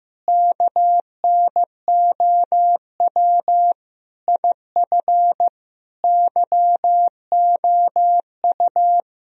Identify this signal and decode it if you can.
Morse code, heard as a keyed tone, about 16 wpm, 700 Hz. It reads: KNOW IF YOU
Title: Speed-Racing - Sets of 5 Words 15wpm